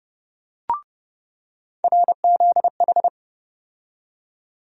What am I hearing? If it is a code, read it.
Morse code, heard as a keyed tone, about 30 wpm, 700 Hz. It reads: RZH
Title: USA Call Sign Postfixes - 3 Letters 30wpm